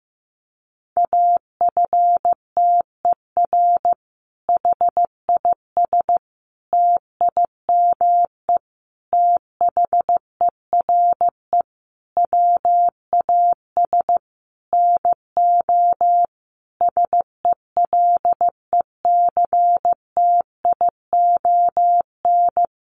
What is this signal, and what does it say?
Morse code, heard as a keyed tone, about 15 wpm, 700 Hz. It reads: AFTER HIS TIME THERE WAS NO SELECTION
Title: Sentences from Top 1400 Words 15wpm